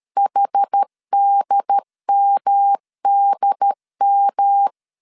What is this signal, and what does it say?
Morse code, heard as a keyed tone, about 13 wpm, 800 Hz. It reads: HDMDM
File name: CW / DET / 1 3 1500C S B 12wpm 2828